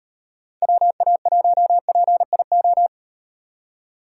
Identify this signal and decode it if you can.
Morse code, heard as a keyed tone, about 38 wpm, 700 Hz. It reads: WA1PIO